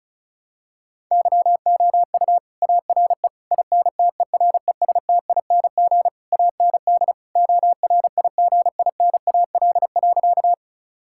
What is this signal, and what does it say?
Morse code, heard as a keyed tone, about 36 wpm, 700 Hz. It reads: YOU ARE INTERESTING AND ORIGINAL.
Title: Sentences from Top 1000 Words - Review Part 6 - 35wpm